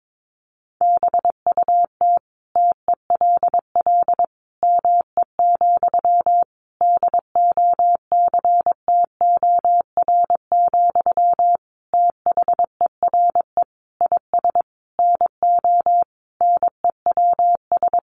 Morse code, heard as a keyed tone, about 22 wpm, 700 Hz. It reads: BUT TELL ME, DOCTOR, THERE IS NO NEWS
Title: Sentences from Top 1500 Words 22wpm